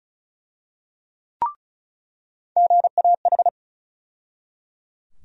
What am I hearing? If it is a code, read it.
Morse code, heard as a keyed tone, about 34 wpm, 700 Hz. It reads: GAH
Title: USA Call Sign Postfixes - 3 Letters 35wpm